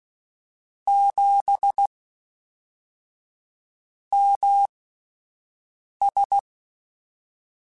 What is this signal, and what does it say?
Morse code, heard as a keyed tone, about 16 wpm, spaced as for 4 wpm, 800 Hz. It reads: 7MS
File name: CW Comparisons / CW Comparisons-mp3s / CW Comparisons Broken Characters @10